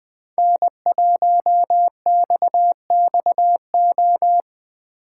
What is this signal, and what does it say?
Morse code, heard as a keyed tone, about 20 wpm, 700 Hz. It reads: N1XXO
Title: USA Call Signs - 5 Characters - 2x Character Spacing - 20wpm